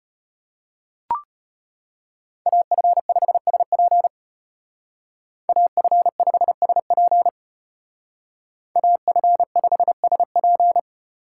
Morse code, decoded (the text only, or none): AF5SP AF5SP AF5SP